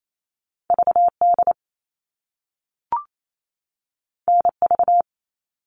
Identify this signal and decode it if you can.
Morse code, heard as a keyed tone, about 28 wpm, 700 Hz. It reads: VD NV
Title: All Cut Numbers - 2-Digit 28wpm